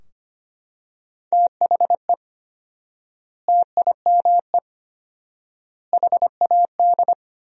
Morse code, heard as a keyed tone, about 25 wpm, 700 Hz. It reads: THE TIME HAD
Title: 4x Word-Spacing - Sentences from Top 100 Words - 25wpm